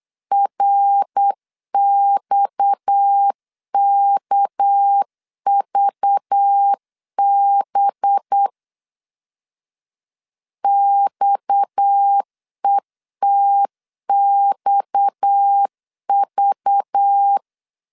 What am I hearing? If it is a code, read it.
Morse code, heard as a keyed tone, about 8 wpm, 800 Hz. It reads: RXKVB XETXV
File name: LEZIONI / 6 PKX0 / 6 PKX 8wpm 3041